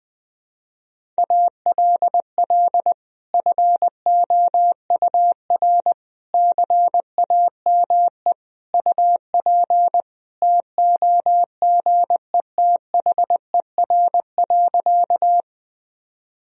Morse code, heard as a keyed tone, about 20 wpm, 700 Hz. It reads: ALL FOUR CAME UP TOGETHER.
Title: Sentences from Top 1000 Words - Review Part 1 - 20wpm